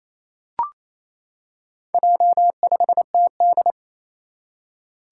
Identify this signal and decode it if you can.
Morse code, heard as a keyed tone, about 27 wpm, 700 Hz. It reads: J5TD
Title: Four Challenging Letter-Number Combinations 28wpm